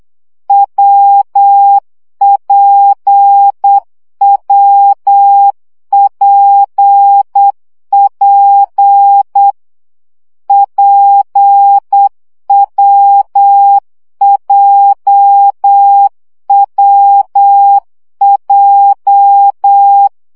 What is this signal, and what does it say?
Morse code, heard as a keyed tone, about 8 wpm, 800 Hz. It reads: WPWPP PWJWJ
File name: CW gyakorlas w p j 7 baud